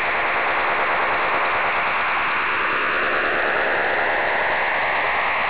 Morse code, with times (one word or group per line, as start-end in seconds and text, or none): none